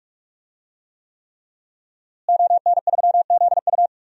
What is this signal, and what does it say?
Morse code, heard as a keyed tone, about 46 wpm, 700 Hz. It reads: ON3ZU
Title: DXCC-100 Call Signs - 5 Characters - Repeated 3x - 45wpm